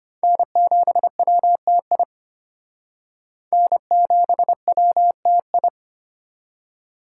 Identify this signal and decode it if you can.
Morse code, heard as a keyed tone, about 33 wpm, 700 Hz. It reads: N7WTI N7WTI